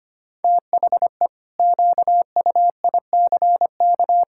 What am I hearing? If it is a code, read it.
Morse code, heard as a keyed tone, about 25 wpm, 700 Hz. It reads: THE QUICK
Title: Pangram Challenge v2 - 25wpm